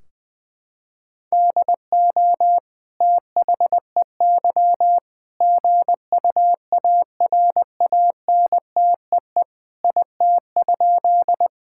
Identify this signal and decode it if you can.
Morse code, heard as a keyed tone, about 20 wpm, 700 Hz. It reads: DO THEY GUARANTEE IT?
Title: Sentences from Top 1600 Words 20wpm